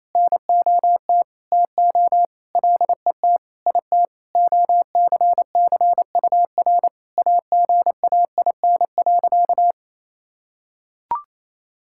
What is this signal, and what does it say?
Morse code, heard as a keyed tone, about 28 wpm, 700 Hz. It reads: NOT TO LET IT OCCUR AGAIN.
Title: Sentences from Top 1000 Words - Review Part 6 - 28wpm